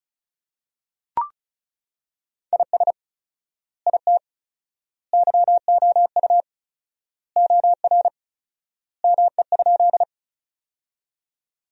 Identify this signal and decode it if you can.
Morse code, heard as a keyed tone, about 34 wpm, 700 Hz. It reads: IS IT YOU OR ME?
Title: 4x Word-Spacing - Sentences from Top 200 Words - 35wpm